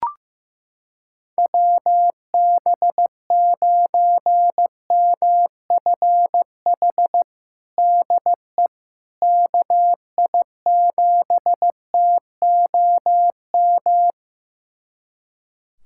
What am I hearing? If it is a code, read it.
Morse code, heard as a keyed tone, about 16 wpm, 700 Hz. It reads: WB9MFH DE KI7TOM
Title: USA Callsign DE Callsign - 6 Characters - 15wpm